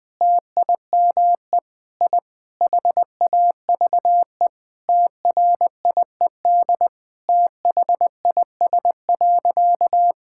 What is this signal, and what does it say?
Morse code, heard as a keyed tone, about 20 wpm, 700 Hz. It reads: TIME I HAVE TRIED THIS.
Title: Sentences with 4-Word Phrases - 20wpm